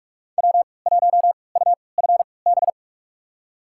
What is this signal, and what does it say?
Morse code, heard as a keyed tone, about 46 wpm, spaced as for 25 wpm, 700 Hz. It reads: W1UFB
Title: USA Call Signs - 5 Characters - 3x Character Spacing - 45wpm